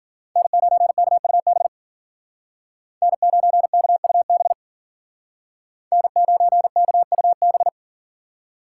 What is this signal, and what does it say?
Morse code, heard as a keyed tone, about 50 wpm, 700 Hz. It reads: N9KUB N9KUB N9KUB
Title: Speed Racing - USA 1x3 Call Signs 40wpm